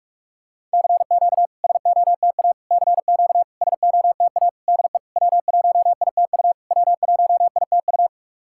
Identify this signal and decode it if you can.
Morse code, heard as a keyed tone, about 44 wpm, 690 Hz. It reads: CQ SOTA CQ SOTA DE W1ITU W1ITU
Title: SOTA QSO Practice 45wpm